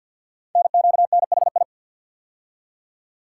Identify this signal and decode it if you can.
Morse code, heard as a keyed tone, about 50 wpm, 700 Hz. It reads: NQNHI
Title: Code Groups - 5 Letters - 50wpm